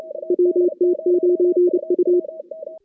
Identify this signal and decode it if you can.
Morse code, heard as a keyed tone, about 26 wpm, 410 Hz. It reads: WT9U
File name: 2016 JAN CW NAQP